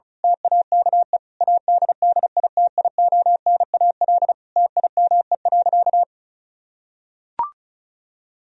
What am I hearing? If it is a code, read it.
Morse code, heard as a keyed tone, about 34 wpm, 700 Hz. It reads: TAKE ADDITIONAL TIME.
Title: Sentences from Top 1000 Words - Review Part 4 - 35wpm